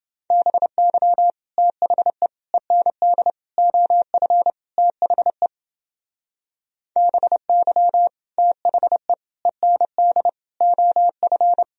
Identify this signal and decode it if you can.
Morse code, heard as a keyed tone, about 28 wpm, 700 Hz. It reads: BY THE END OF THE BY THE END OF